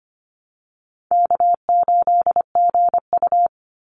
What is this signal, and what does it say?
Morse code, heard as a keyed tone, about 25 wpm, 700 Hz. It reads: K8GU
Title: Easy Call Signs Repeated 2x - 25wpm